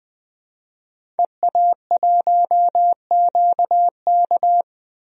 Morse code, heard as a keyed tone, about 20 wpm, 700 Hz. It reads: EA1QK
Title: DXCC-100 Call Signs - 5 Characters - Repeated 3x - 20wpm